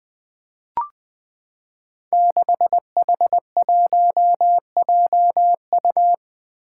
Morse code, heard as a keyed tone, about 20 wpm, 700 Hz. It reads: 6H1JU
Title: Five Challenging Letter-Number Combinations 20wpm